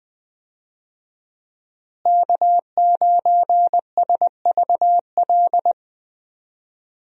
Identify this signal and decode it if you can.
Morse code, heard as a keyed tone, about 20 wpm, 700 Hz. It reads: K9SVL
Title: USA Call Signs - 5 Characters - Repeated 3x - 20wpm